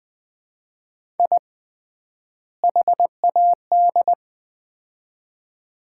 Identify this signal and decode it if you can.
Morse code, heard as a keyed tone, about 20 wpm, 700 Hz. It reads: I HAD
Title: 3x Word-Spacing - Sets of 2 Words - 20wpm